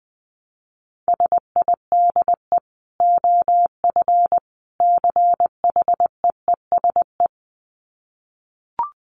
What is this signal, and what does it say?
Morse code, heard as a keyed tone, about 20 wpm, 700 Hz. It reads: SIDE OF CHEESE
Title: Dit Challenge 20wpm